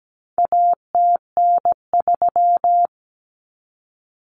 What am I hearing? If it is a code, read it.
Morse code, heard as a keyed tone, about 17 wpm, 700 Hz. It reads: ATN3
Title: Common Cut Numbers - 4-Digit 17wpm